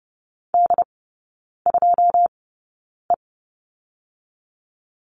Morse code, heard as a keyed tone, about 30 wpm, spaced as for 9 wpm, 700 Hz. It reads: D2E
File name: 081 Lesson 14e - Number 2 - Three Characters 30wpm